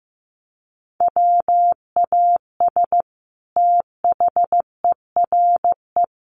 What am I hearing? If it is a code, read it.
Morse code, heard as a keyed tone, about 15 wpm, 700 Hz. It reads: WAS THERE
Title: Sentences from Top 1400 Words 15wpm